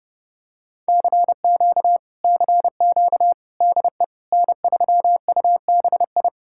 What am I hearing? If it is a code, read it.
Morse code, heard as a keyed tone, about 30 wpm, 700 Hz. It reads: CQ CQ DE N3UBI